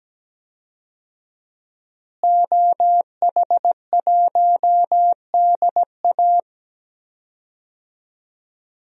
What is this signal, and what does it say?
Morse code, heard as a keyed tone, about 17 wpm, 700 Hz. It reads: OH1DA
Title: DXCC-100 Call Signs - 5 Characters - Repeated 2x - 17wpm